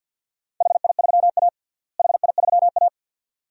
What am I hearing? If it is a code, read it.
Morse code, heard as a keyed tone, about 53 wpm, 700 Hz. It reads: HI3A HI3A
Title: DXCC-100 Call Signs - 4 Characters - Repeated 2x - 50wpm